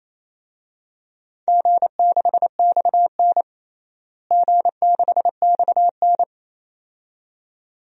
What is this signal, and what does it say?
Morse code, heard as a keyed tone, about 28 wpm, 700 Hz. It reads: G6XN G6XN
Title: DXCC-100 Call Signs - 4 Characters - Repeated 2x - 28wpm